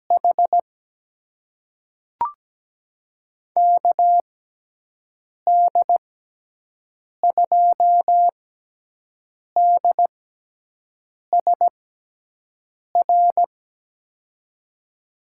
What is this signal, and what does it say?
Morse code, heard as a keyed tone, about 17 wpm, spaced as for 6 wpm, 700 Hz. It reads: H KD2DSR
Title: USA Call Signs - 6 Characters - 6x Character Spacing - 17wpm